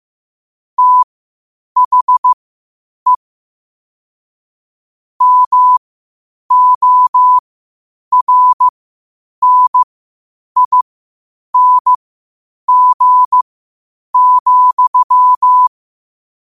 Morse code, heard as a keyed tone, about 15 wpm, spaced as for 8 wpm, 1000 Hz. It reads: THE MORNING,
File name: qotdfiles / 8wpm